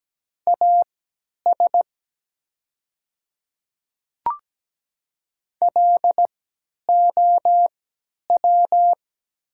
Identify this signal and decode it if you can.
Morse code, heard as a keyed tone, about 17 wpm, spaced as for 10 wpm, 700 Hz. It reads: AS LOW
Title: 3x Character Spacing - Top 200 Words - 17wpm